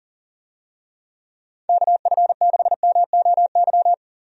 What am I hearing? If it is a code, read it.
Morse code, heard as a keyed tone, about 40 wpm, 700 Hz. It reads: KF6MOY
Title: USA Call Signs - 6 Characters - 40wpm